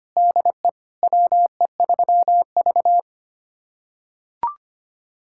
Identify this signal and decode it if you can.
Morse code, heard as a keyed tone, about 25 wpm, 700 Hz. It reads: DE WE3V
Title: USA Callsign DE Callsign - 4 Characters - 25wpm